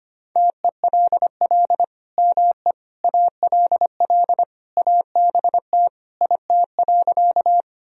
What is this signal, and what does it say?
Morse code, heard as a keyed tone, about 25 wpm, 700 Hz. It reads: TELL ME ALL ABT IT.
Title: Sentences from Top 200 Words and Abbreviations 25wpm